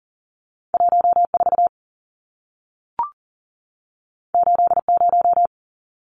Morse code, heard as a keyed tone, about 40 wpm, 700 Hz. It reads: 14 80
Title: Numbers - 2-Digit 40wpm